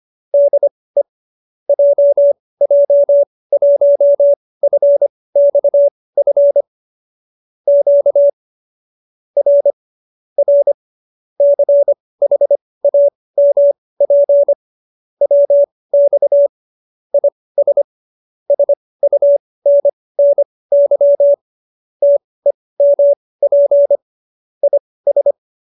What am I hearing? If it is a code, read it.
Morse code, heard as a keyed tone, about 25 wpm, spaced as for 18 wpm, 570 Hz. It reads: DE JJ1FXF Q R R CHAMP WX IS SUNNY TEMP IS